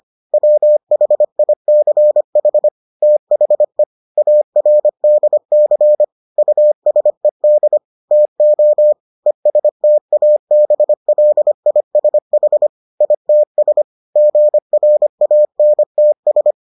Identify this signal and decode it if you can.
Morse code, heard as a keyed tone, about 25 wpm, 600 Hz. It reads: WHICH THE ARDC USED TO ESTABLISH ITS GRANTS